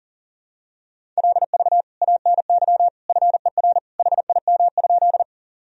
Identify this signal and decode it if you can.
Morse code, heard as a keyed tone, about 40 wpm, 700 Hz. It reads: LV ANY FER HIM?
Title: Sentences from Top 400 Words and Abbreviations 40wpm